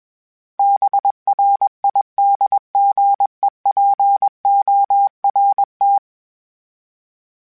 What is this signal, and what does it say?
Morse code, heard as a keyed tone, about 21 wpm, 800 Hz. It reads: BRIDGEPORT
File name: fromSMs / FromSMs mp3s / Cities@20